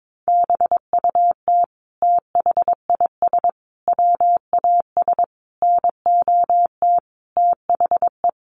Morse code, heard as a keyed tone, about 22 wpm, 700 Hz. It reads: BUT THIS WAS NOT THE
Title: Sentences from Top 700 Words 22wpm